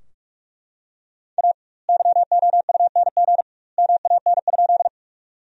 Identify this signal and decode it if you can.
Morse code, heard as a keyed tone, about 41 wpm, 700 Hz. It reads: A YOUNG MAN?
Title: 2x Word-Spacing - Sentences from Top 400 Words - 45wpm